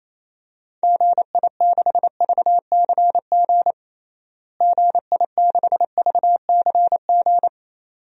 Morse code, heard as a keyed tone, about 28 wpm, 700 Hz. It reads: GI6VCG GI6VCG
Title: DXCC-100 Call Signs - 6 Characters - Repeated 2x - 28wpm